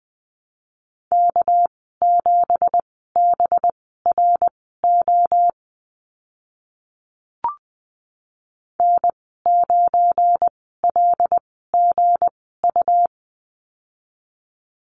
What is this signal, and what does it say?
Morse code, heard as a keyed tone, about 20 wpm, spaced as for 14 wpm, 700 Hz. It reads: K7BRO N9LGU
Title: 245 Lesson 39g - Letter z - Call Signs 20wpm